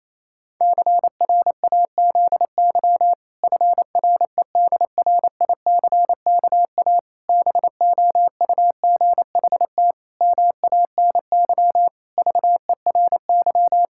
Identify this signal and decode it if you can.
Morse code, heard as a keyed tone, about 28 wpm, 700 Hz. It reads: CRAZY FREDRICKA BOUGHT MANY VERY
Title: Pangram Challenge v2 - 28wpm